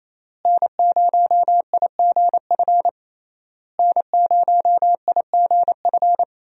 Morse code, heard as a keyed tone, about 28 wpm, 700 Hz. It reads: N0IGF N0IGF
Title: USA Call Signs - 5 Characters - Repeated 3x - 28wpm